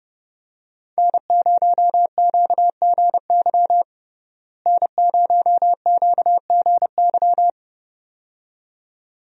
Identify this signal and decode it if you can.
Morse code, heard as a keyed tone, about 30 wpm, 700 Hz. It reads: N0QGY N0QGY
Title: USA Call Signs - 5 Characters - Repeated 2x - 30wpm